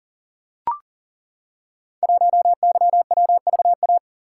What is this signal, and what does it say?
Morse code, heard as a keyed tone, about 38 wpm, 700 Hz. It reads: 1YWVA
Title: Code Groups - 5 Letter:Numbers - 40wpm